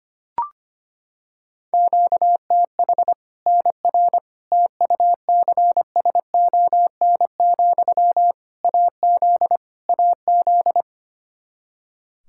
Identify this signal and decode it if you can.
Morse code, heard as a keyed tone, about 25 wpm, 700 Hz. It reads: QTH NR TUCSON, AZ AZ
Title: Practice QSO - Exchange 1 - Beginning - 25wpm